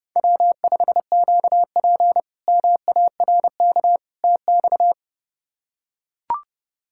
Morse code, heard as a keyed tone, about 30 wpm, 700 Hz. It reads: W5QP MARK TX
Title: K1USN SST QSO Practice - 30wpm